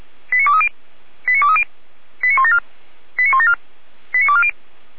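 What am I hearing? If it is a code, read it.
Five-tone selective call (ZVEI), heard as quick runs of five tones, tones of about 70 ms. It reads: 89230 89230 89174 89174 89240
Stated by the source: ZVEI five-tone call 89167 received by SDR, 455 channel, Feuerwehr Orlamünde.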